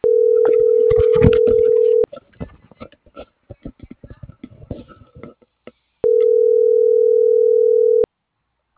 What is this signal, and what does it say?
Ringback tone (North America), 440 + 480 Hz.